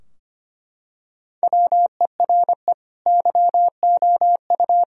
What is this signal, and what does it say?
Morse code, heard as a keyed tone, about 25 wpm, 700 Hz. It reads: WERE YOU
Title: Sentences from Top 1000 Words - Review Part 4 - 25wpm